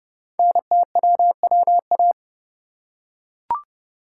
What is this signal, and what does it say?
Morse code, heard as a keyed tone, about 30 wpm, 690 Hz. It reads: NTWWA
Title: Code Groups - 5 Letters - 30wpm